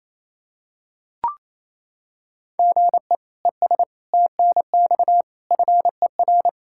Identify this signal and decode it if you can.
Morse code, heard as a keyed tone, about 28 wpm, 700 Hz. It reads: GE ES TNX FER
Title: K1USN SST QSO Practice 28wpm